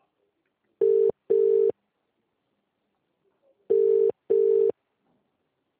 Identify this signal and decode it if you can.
Ringback tone (UK), 400 + 450 Hz.